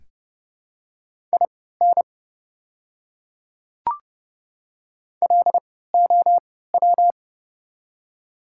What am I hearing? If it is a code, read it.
Morse code, heard as a keyed tone, about 30 wpm, spaced as for 17 wpm, 700 Hz. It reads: IN LOW
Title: 3x Character Spacing - Top 200 Words - 30wpm